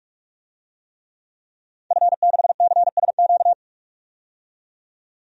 Morse code, heard as a keyed tone, about 45 wpm, 700 Hz. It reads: F6CSQ
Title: DXCC-100 Call Signs - 5 Characters - 45wpm